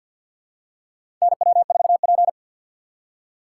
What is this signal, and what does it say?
Morse code, heard as a keyed tone, about 50 wpm, 700 Hz. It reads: NW4P
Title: USA Call Signs - 4 Characters - 6x Character Spacing - 50wpm